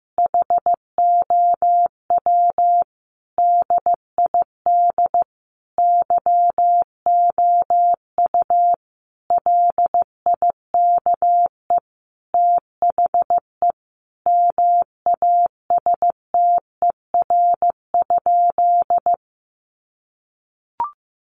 Morse code, heard as a keyed tone, about 15 wpm, 700 Hz. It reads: HOW DID YOU LIKE THE MASTER?
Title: Sentences from Top 1300 Words 15wpm